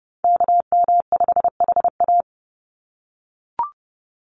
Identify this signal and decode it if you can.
Morse code, heard as a keyed tone, about 30 wpm, 700 Hz. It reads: KM5HA
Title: USA Call Signs 30wpm